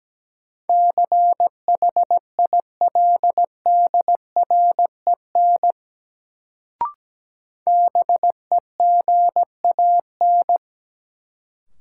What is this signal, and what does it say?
Morse code, heard as a keyed tone, about 17 wpm, 700 Hz. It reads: CHILDREN BEGAN
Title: Sentences from Top 300 Words - New Word Drill - 17wpm